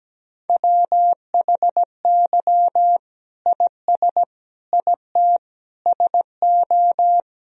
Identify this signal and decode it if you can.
Morse code, heard as a keyed tone, about 17 wpm, 700 Hz. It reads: WHY IS IT SO
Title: Sentences from Top 1000 Words - Review Part 1 - 17wpm